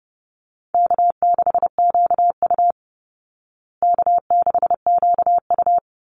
Morse code, heard as a keyed tone, about 28 wpm, 700 Hz. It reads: K6QU K6QU
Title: Call Signs Repeated 2x - 30wpm